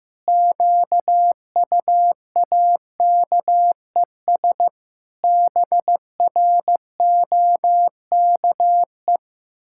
Morse code, heard as a keyed tone, about 14 wpm, 700 Hz. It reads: QUAKES BROKE
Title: Pangram Challenge v2 - 15wpm